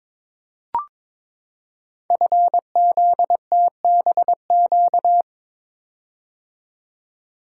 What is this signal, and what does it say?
Morse code, heard as a keyed tone, about 23 wpm, 700 Hz. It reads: FZTBQ
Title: Code Groups - 5 Letters - 22wpm